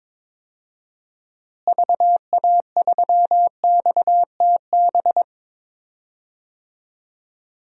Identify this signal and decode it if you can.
Morse code, heard as a keyed tone, about 22 wpm, 700 Hz. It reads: VA3XTB
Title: DXCC-100 Call Signs - 6 Characters - Repeated 3x - 22wpm